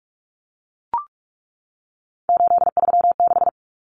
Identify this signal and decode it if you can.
Morse code, heard as a keyed tone, about 44 wpm, 700 Hz. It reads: 836
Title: Numbers - 3-Digit 45wpm